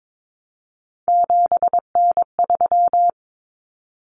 Morse code, heard as a keyed tone, about 22 wpm, 700 Hz. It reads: 7N3
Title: Common Cut Numbers - 3-Digit 22wpm